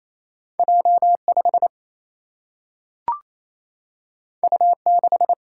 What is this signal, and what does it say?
Morse code, heard as a keyed tone, about 28 wpm, 700 Hz. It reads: J5 U6
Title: Single-Letter Single-Number 28wpm